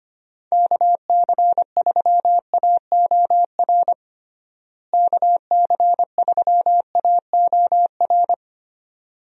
Morse code, heard as a keyed tone, about 25 wpm, 700 Hz. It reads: KC3AOR KC3AOR
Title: USA Call Signs - 6 Characters - Repeated 2x - 25wpm